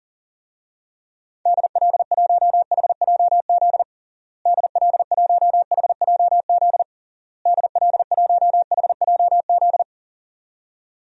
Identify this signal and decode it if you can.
Morse code, heard as a keyed tone, about 40 wpm, 700 Hz. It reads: DL1HJZ DL1HJZ DL1HJZ
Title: DXCC-100 Call Signs - 6 Characters - Repeated 3x - 40wpm